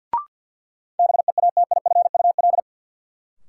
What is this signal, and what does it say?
Morse code, heard as a keyed tone, about 50 wpm, 700 Hz. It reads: BEATIFUL
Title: Sentences from Top 1300 Words - New Word Drill - 50wpm